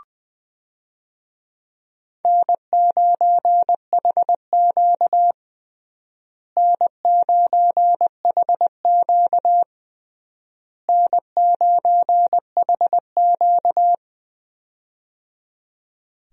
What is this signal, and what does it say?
Morse code, heard as a keyed tone, about 20 wpm, 700 Hz. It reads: N9HQ N9HQ N9HQ
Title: USA Call Signs - 4 Characters - Repeated 3x - 20wpm